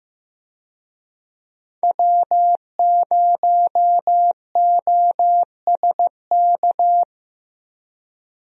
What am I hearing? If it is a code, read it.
Morse code, heard as a keyed tone, about 15 wpm, 700 Hz. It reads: W0OSK
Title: USA Call Signs - 5 Characters - 15wpm